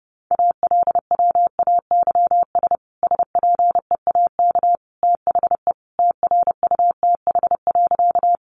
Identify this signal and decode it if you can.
Morse code, heard as a keyed tone, about 30 wpm, 700 Hz. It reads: ALWAYS SPEAK THE TRUTH.
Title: Sentences from Top 1100 Words 30wpm